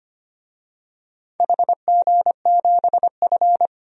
Morse code, heard as a keyed tone, about 25 wpm, 700 Hz. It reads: HG7F
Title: DXCC-100 Call Signs - 4 Characters - 25wpm